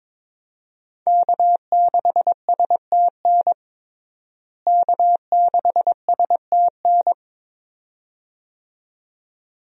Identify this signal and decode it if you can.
Morse code, heard as a keyed tone, about 21 wpm, 700 Hz. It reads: K6STN K6STN
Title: USA Call Signs - 5 Characters - Repeated 2x - 22wpm